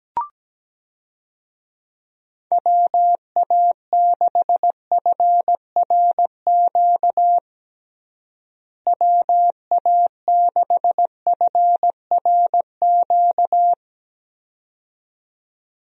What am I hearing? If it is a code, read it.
Morse code, heard as a keyed tone, about 17 wpm, 700 Hz. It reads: WA6FRQ WA6FRQ
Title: USA Call Signs - 6 Characters - Repeated 2x - 17wpm